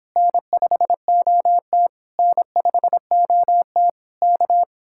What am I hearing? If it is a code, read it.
Morse code, heard as a keyed tone, about 26 wpm, 700 Hz. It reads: N5OT N5OT K